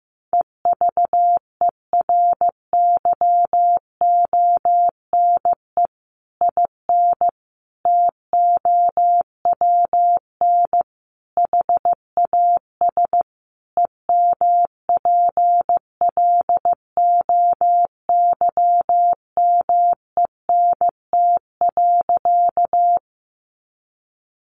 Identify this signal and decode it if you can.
Morse code, heard as a keyed tone, about 15 wpm, 700 Hz. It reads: EVERYONE IN TOWN HAS EMPLOYMENT.
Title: Sentences from Top 1500 Words 15wpm